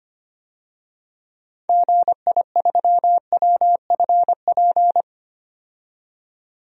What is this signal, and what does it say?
Morse code, heard as a keyed tone, about 25 wpm, 700 Hz. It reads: GI3WFP
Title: DXCC-100 Call Signs - 6 Characters - 25wpm